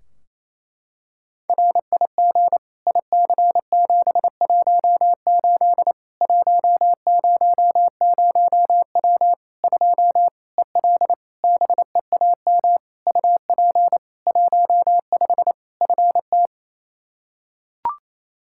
Morse code, heard as a keyed tone, about 27 wpm, 700 Hz. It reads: RIG IC718 100W 2 EL BEAM UP 15 FT
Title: Practice QSO - Exchange 2 - Info - 28wpm